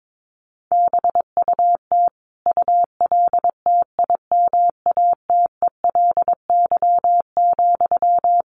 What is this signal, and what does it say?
Morse code, heard as a keyed tone, about 22 wpm, 700 Hz. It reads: BUT ULTIMATELY,
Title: Sentences from Top 1200 Words 22wpm